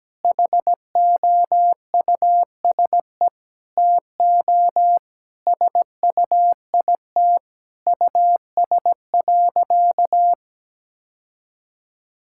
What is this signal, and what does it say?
Morse code, heard as a keyed tone, about 17 wpm, 700 Hz. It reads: HOUSE TO SUIT US.
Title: Sentences from Top 1600 Words 17wpm